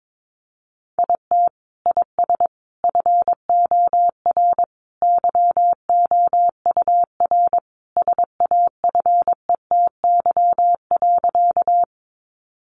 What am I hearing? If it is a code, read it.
Morse code, heard as a keyed tone, about 22 wpm, 700 Hz. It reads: IT IS FOR YOUR SAFETY.
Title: Sentences from Top 700 Words 22wpm